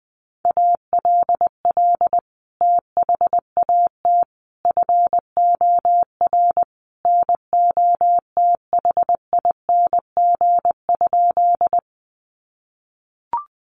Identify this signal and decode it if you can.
Morse code, heard as a keyed tone, about 20 wpm, 700 Hz. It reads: ALL THAT FOR NOTHING?
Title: Sentences from Top 500 Words 20wpm